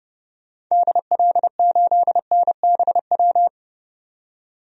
Morse code, heard as a keyed tone, about 30 wpm, 700 Hz. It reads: DL8NBW
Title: DXCC-100 Call Signs - 6 Characters - 30wpm